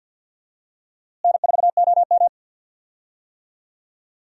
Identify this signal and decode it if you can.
Morse code, heard as a keyed tone, about 51 wpm, 700 Hz. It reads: N4OM
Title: USA Call Signs - 4 Characters - Repeated 2x - 50wpm